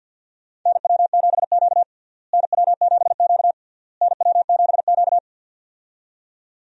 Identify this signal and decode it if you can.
Morse code, heard as a keyed tone, about 52 wpm, 700 Hz. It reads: NW7Q NW7Q NW7Q
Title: USA Call Signs - 4 Characters - Repeated 3x - 50wpm